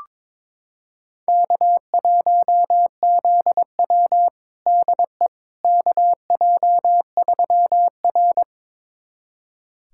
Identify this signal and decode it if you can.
Morse code, heard as a keyed tone, about 22 wpm, 700 Hz. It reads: K1ZW DE KJ3R
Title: USA Callsign DE Callsign - 4 Characters - 22wpm